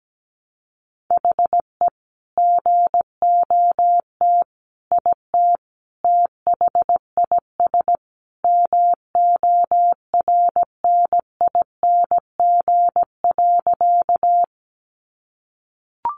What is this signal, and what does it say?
Morse code, heard as a keyed tone, about 17 wpm, 700 Hz. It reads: HE GOT IT THIS MORNING.
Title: Sentences from Top 400 Words 17wpm